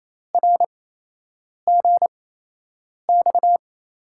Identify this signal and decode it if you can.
Morse code, heard as a keyed tone, about 28 wpm, spaced as for 8 wpm, 700 Hz. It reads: RGX
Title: USA Call Signs - 6 Characters - 8x Character Spacing - 28wpm